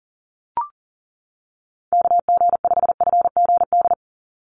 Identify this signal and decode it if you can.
Morse code, heard as a keyed tone, about 40 wpm, 700 Hz. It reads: KG5FGD
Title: USA Call Signs 40wpm